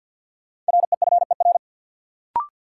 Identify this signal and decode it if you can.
Morse code, heard as a keyed tone, about 52 wpm, 700 Hz. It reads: REFER
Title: Sentences from Top 1000 Words - New Word Drill - 50wpm